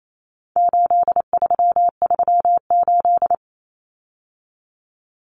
Morse code, heard as a keyed tone, about 28 wpm, 700 Hz. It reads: 8338
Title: Numbers - 4-Digit 28wpm